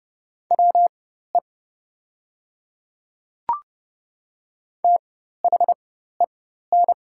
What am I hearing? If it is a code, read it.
Morse code, heard as a keyed tone, about 30 wpm, spaced as for 14 wpm, 700 Hz. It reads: WE THEN
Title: 4x Character Spacing - Top 300 Words - 30wpm